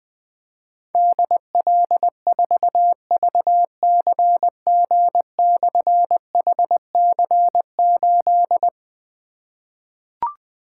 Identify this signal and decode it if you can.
Morse code, heard as a keyed tone, about 20 wpm, 700 Hz. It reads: DL4VCG/HC8
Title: Difficult Call Signs 20wpm